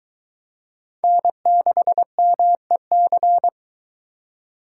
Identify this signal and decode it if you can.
Morse code, heard as a keyed tone, about 22 wpm, 700 Hz. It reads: N6MEC